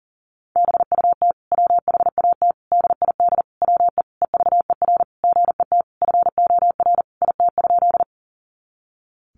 Morse code, heard as a keyed tone, about 40 wpm, 700 Hz. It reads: BUT WHAT DID WE EVER GET FOR IT?
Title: Sentences from Top 400 Words 40wpm